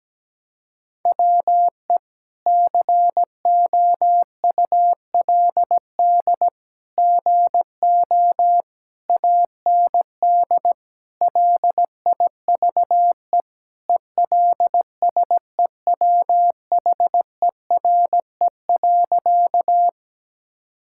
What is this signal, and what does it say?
Morse code, heard as a keyed tone, about 17 wpm, 700 Hz. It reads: WE COULD GO AND LIVE ELSEWHERE.
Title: Sentences from Top 1600 Words 17wpm